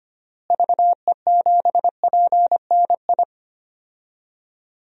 Morse code, heard as a keyed tone, about 25 wpm, 700 Hz. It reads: VE7PNI
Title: DXCC-100 Call Signs - 6 Characters - Repeated 3x - 25wpm